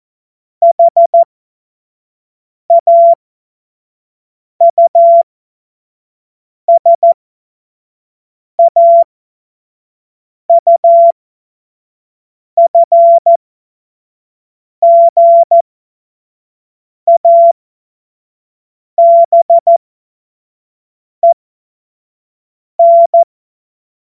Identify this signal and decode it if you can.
Morse code, heard as a keyed tone, about 14 wpm, spaced as for 5 wpm, 680 Hz. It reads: HAUSAUFGABEN
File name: Morsekurs / Witze / Witz 15. Juni 2021, 25 F70